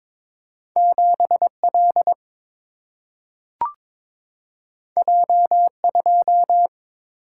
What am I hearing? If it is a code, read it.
Morse code, heard as a keyed tone, about 22 wpm, 700 Hz. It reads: 7L J2
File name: practice / 22wpm / Two Letter-Number Combinations - 22wpm